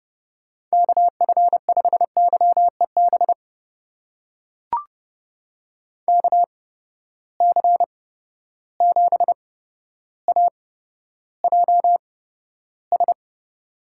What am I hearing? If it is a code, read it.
Morse code, heard as a keyed tone, about 32 wpm, spaced as for 13 wpm, 700 Hz. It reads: KF5YEB KC7AJS
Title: USA Call Signs - 6 Characters - 8x Character Spacing - 30wpm